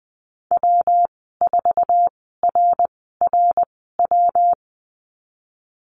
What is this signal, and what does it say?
Morse code, heard as a keyed tone, about 20 wpm, spaced as for 15 wpm, 700 Hz. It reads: W4RRW
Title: 167 Lesson 27g - Number 7 - Call Signs 20wpm